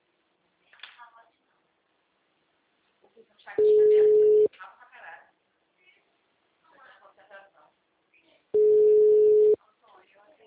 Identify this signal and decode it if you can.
Ringback tone (Europe, 425 Hz), with speech.